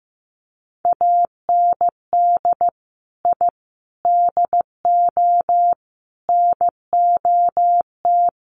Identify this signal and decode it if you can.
Morse code, heard as a keyed tone, about 15 wpm, 700 Hz. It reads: AND I DO NOT
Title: Sets of 5 Words Encore 15wpm